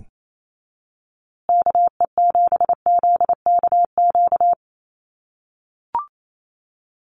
Morse code, heard as a keyed tone, about 28 wpm, 700 Hz. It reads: KE7ZKQ
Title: USA Call Signs Repeated 2x 28wpm